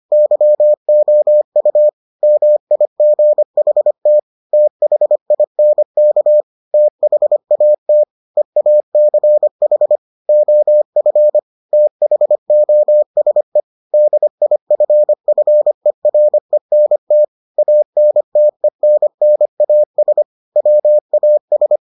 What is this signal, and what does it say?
Morse code, heard as a keyed tone, about 25 wpm, 600 Hz. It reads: YOU MIGHT THINK THAT EACH OF THOSE DIFFERENT ANTENNAS WAS